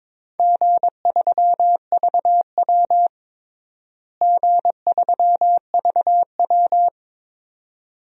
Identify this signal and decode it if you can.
Morse code, heard as a keyed tone, about 22 wpm, 700 Hz. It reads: G3VW G3VW
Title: DXCC-100 Call Signs - 4 Characters - Repeated 2x - 22wpm